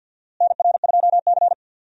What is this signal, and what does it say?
Morse code, heard as a keyed tone, about 47 wpm, 700 Hz. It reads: NR2C
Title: USA Call Signs - 4 Characters - Repeated 3x - 50wpm